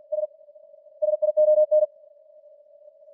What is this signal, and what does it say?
Morse code, heard as a keyed tone, about 44 wpm, spaced as for 25 wpm, 610 Hz. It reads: SSION